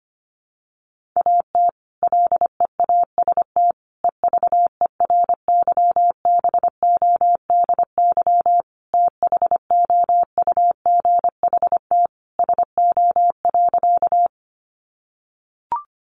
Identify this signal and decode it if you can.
Morse code, heard as a keyed tone, about 25 wpm, 700 Hz. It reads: AT LEAST EVERYBODY THOUGHT SO.
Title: Sentences from Top 1400 Words 25wpm